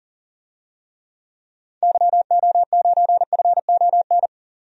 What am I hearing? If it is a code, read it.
Morse code, heard as a keyed tone, about 39 wpm, 700 Hz. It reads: YO9FON
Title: DXCC-100 Call Signs - 6 Characters - 40wpm